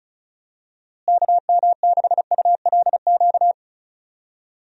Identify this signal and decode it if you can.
Morse code, heard as a keyed tone, about 36 wpm, 700 Hz. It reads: KM6ULQ
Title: USA Call Signs - 6 Characters - Repeated 3x - 35wpm